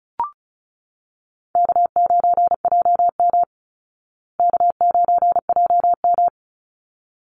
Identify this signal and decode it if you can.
Morse code, heard as a keyed tone, about 34 wpm, 700 Hz. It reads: K9JM K9JM
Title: Easy Call Signs Repeated 2x - 35wpm